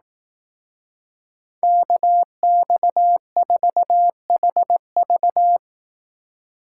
Morse code, heard as a keyed tone, about 18 wpm, 700 Hz. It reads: KX4HV
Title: Speed Racing - USA 2x2 Call Signs 15wpm